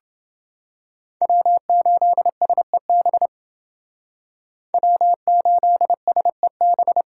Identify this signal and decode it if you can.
Morse code, heard as a keyed tone, about 28 wpm, 700 Hz. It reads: W8SEB W8SEB